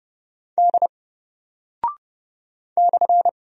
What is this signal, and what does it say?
Morse code, heard as a keyed tone, about 30 wpm, 700 Hz. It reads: D /
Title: Single Character (Letter:Number:Punctuation:Prosign) 30wpm